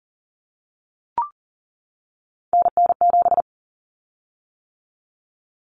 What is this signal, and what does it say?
Morse code, heard as a keyed tone, about 40 wpm, 700 Hz. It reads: NN7
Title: Common Cut Numbers - 3-Digit 40wpm